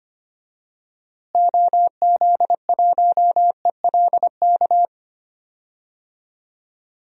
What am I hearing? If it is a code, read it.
Morse code, heard as a keyed tone, about 25 wpm, 700 Hz. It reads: OZ1ELK
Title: DXCC-100 Call Signs - 6 Characters - Repeated 2x - 25wpm